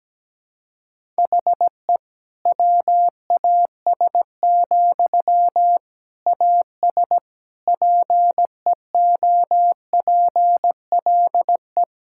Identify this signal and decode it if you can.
Morse code, heard as a keyed tone, about 17 wpm, 700 Hz. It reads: HE WAS, AS PEOPLE
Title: Sentences from Top 1000 Words - Review Part 6 - 17wpm